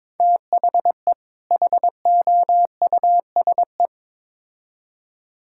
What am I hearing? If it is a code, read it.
Morse code, heard as a keyed tone, about 22 wpm, 700 Hz. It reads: THE HOUSE
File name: Build A Phrase - 2-Word Phrases - 22wpm